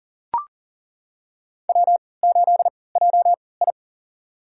Morse code, heard as a keyed tone, about 40 wpm, spaced as for 23 wpm, 700 Hz. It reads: W8JI
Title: USA Call Signs - 4 Characters - 3x Character Spacing - 40wpm